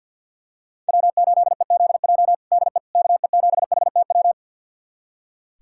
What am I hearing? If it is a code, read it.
Morse code, heard as a keyed tone, about 46 wpm, 700 Hz. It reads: W8EZJ DE KE7HTW
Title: USA Callsign DE Callsign 50wpm